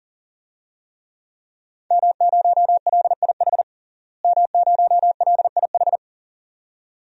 Morse code, heard as a keyed tone, about 40 wpm, 700 Hz. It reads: M0LIH M0LIH
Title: DXCC-100 Call Signs - 5 Characters - Repeated 2x - 40wpm